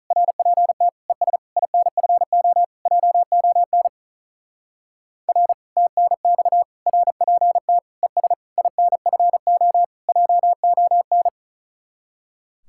Morse code, heard as a keyed tone, about 43 wpm, 700 Hz. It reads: RPT ES INFO JON R TNX RPT ES INFO JON
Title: Speed-Racing Common QSO Phrases - 35wpm